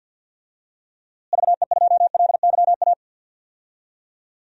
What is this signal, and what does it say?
Morse code, heard as a keyed tone, about 52 wpm, 700 Hz. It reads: VE2LYA
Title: DXCC-100 Call Signs - 6 Characters - Repeated 2x - 50wpm